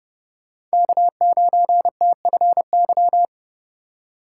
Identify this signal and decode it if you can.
Morse code, heard as a keyed tone, about 31 wpm, 700 Hz. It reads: K9TFY
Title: Speed Racing - USA 1x3 Call Signs 25wpm